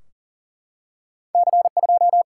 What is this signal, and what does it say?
Morse code, heard as a keyed tone, about 42 wpm, 700 Hz. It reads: C2
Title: Single-Letter Single-Number 40wpm